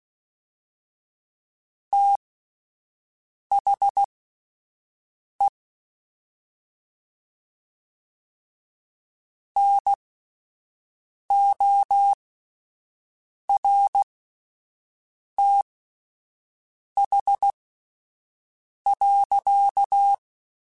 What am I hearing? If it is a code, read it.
Morse code, heard as a keyed tone, about 16 wpm, spaced as for 5 wpm, 800 Hz. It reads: THE NORTH.
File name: Spaced Characters / Spaced Characters-mp3s / Weather4 24 sp @10